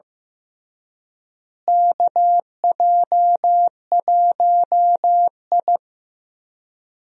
Morse code, heard as a keyed tone, about 15 wpm, 700 Hz. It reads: KJ1I
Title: USA Call Signs - 4 Characters - Repeated 3x - 15wpm